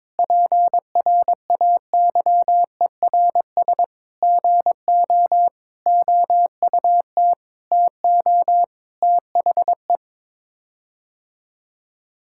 Morse code, heard as a keyed tone, about 22 wpm, 700 Hz. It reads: PRAYERS GO OUT TO THE
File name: practice / 22wpm / Sets of 8 Words - 22wpm